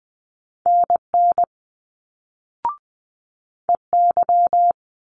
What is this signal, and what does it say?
Morse code, heard as a keyed tone, about 20 wpm, 700 Hz. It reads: NN EY
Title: Multi-letter Phonemes Quick Fire 20wpm